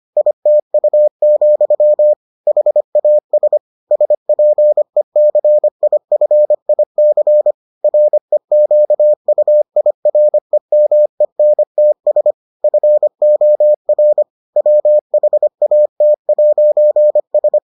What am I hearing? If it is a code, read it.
Morse code, heard as a keyed tone, about 25 wpm, 600 Hz. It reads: ITU, HAS SPECIFIC REQUIREMENTS FOR WHAT'S